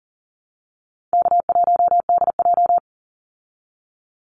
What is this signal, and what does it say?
Morse code, heard as a keyed tone, about 40 wpm, 700 Hz. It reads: K1DJ
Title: Easy Call Signs Repeated 3x - 40wpm